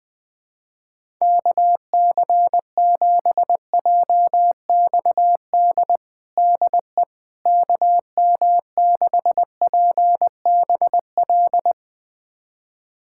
Morse code, heard as a keyed tone, about 21 wpm, 700 Hz. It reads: KC7JXD DE KM6PBL
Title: USA Callsign DE Callsign - 6 Characters - 20wpm